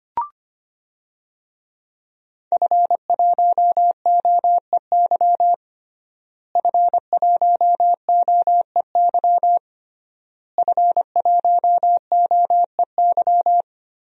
Morse code, heard as a keyed tone, about 25 wpm, 700 Hz. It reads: F1OEY F1OEY F1OEY
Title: DXCC-100 Call Signs - 5 Characters - Repeated 3x - 25wpm